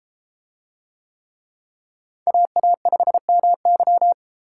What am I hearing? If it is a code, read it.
Morse code, heard as a keyed tone, about 33 wpm, 700 Hz. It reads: AA5MY